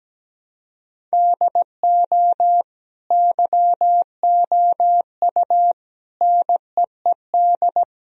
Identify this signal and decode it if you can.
Morse code, heard as a keyed tone, about 17 wpm, 700 Hz. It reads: DO YOU NEED
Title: Sentences from Top 1000 Words - Review Part 1 - 17wpm